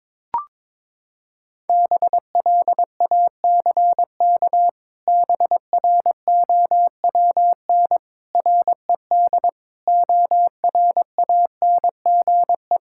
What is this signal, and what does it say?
Morse code, heard as a keyed tone, about 21 wpm, 700 Hz. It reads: BLACK BROWN RED ORANGE
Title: Pangram Challenge v2 - 22wpm